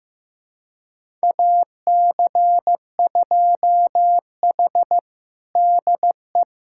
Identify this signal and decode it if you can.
Morse code, heard as a keyed tone, about 15 wpm, 700 Hz. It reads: AC2H DE
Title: USA Callsign DE Callsign - 4 Characters - 15wpm